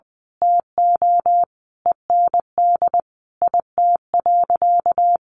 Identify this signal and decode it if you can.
Morse code, heard as a keyed tone, about 20 wpm, 700 Hz. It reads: TO END IT.
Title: Sentences from Top 1200 Words 20wpm